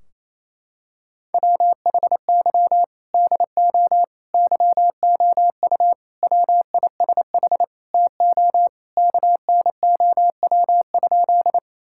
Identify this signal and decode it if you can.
Morse code, heard as a keyed tone, about 27 wpm, 700 Hz. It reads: WHY DO YOU WISH TO KNOW?
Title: Sentences from Top 1600 Words 28wpm